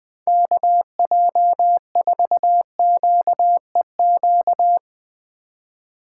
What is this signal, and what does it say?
Morse code, heard as a keyed tone, about 20 wpm, 700 Hz. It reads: KJ4QEQ
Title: Speed Racing - USA 2x3 Call Signs 20wpm